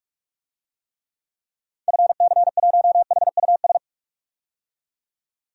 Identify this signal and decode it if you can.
Morse code, heard as a keyed tone, about 42 wpm, 700 Hz. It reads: FC1HUS
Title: DXCC-100 Call Signs - 6 Characters - 45wpm